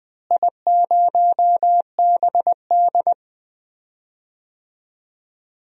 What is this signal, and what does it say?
Morse code, heard as a keyed tone, about 20 wpm, 700 Hz. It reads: I0BD
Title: DXCC-100 Call Signs - 4 Characters - Repeated 2x - 20wpm